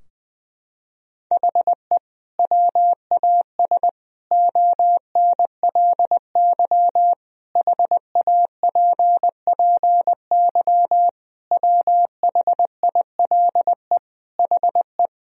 Morse code, heard as a keyed tone, about 20 wpm, 700 Hz. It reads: HE WAS ONLY HAPPY WHILE HE
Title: Sentences from Top 1000 Words - Review Part 4 - 20wpm